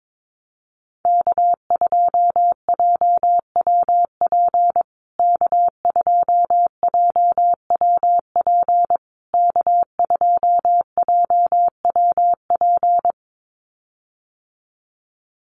Morse code, heard as a keyed tone, about 22 wpm, 690 Hz. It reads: K2JWP K2JWP K2JWP
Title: USA Call Signs Repeated 3x 22wpm